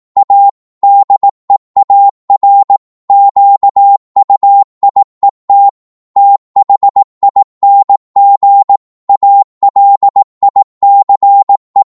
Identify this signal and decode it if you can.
Morse code, heard as a keyed tone, about 18 wpm, 800 Hz. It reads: A DEAR QUIET THING ALICE
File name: Morse / psg-18wpm